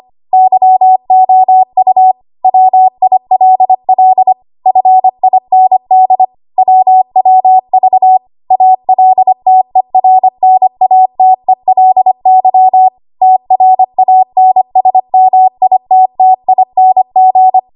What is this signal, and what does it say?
Morse code, heard as a keyed tone, about 25 wpm, 750 Hz. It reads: YOU WILL FIND WWV ALTERNATELY TRANSMITTING
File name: Archive / 25 WPM / 25WPM